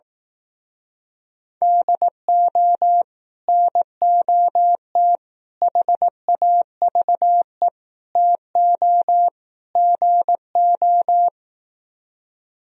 Morse code, heard as a keyed tone, about 18 wpm, 700 Hz. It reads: DO NOT HAVE TO GO
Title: Speed-Racing - Sets of 5 Words 15wpm